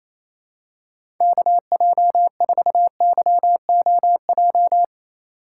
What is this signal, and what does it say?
Morse code, heard as a keyed tone, about 28 wpm, 700 Hz. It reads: KJ4YOJ
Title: USA Call Signs - 6 Characters - 3x Character Spacing - 28wpm